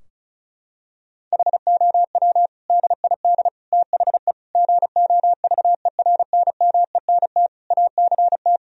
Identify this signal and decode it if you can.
Morse code, heard as a keyed tone, about 34 wpm, 700 Hz. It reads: HOW DID THE GOVERNMENT ACT
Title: Sentences from Top 1000 Words - Review Part 4 - 35wpm